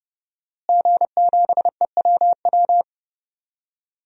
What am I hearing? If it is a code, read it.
Morse code, heard as a keyed tone, about 32 wpm, 700 Hz. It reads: G7EWW
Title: DXCC-100 Call Signs - 5 Characters - Repeated 2x - 30wpm